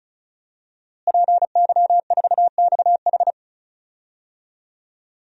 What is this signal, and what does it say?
Morse code, heard as a keyed tone, about 34 wpm, 700 Hz. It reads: PY4XH
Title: DXCC-100 Call Signs - 5 Characters - Repeated 3x - 35wpm